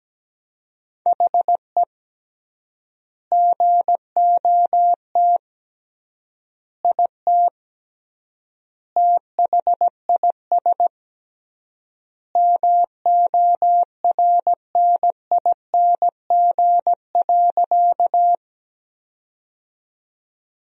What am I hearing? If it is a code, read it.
Morse code, heard as a keyed tone, about 17 wpm, 700 Hz. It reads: HE GOT IT THIS MORNING.
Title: 3x Word-Spacing - Sentences from Top 400 Words - 17wpm